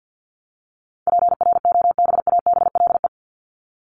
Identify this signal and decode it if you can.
Morse code, heard as a keyed tone, about 50 wpm, 700 Hz. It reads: PROBABLE
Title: LY Suffix - Top 100 Words with Base Words 50wpm